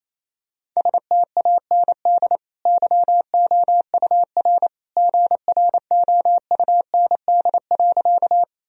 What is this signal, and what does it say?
Morse code, heard as a keyed tone, about 28 wpm, 690 Hz. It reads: STAND YOUR GROUND.
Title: Sentences from Top 1000 Words - Review Part 2 - 28wpm